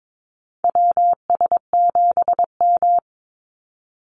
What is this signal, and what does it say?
Morse code, heard as a keyed tone, about 22 wpm, 700 Hz. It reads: WS7M
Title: Call Signs - 22wpm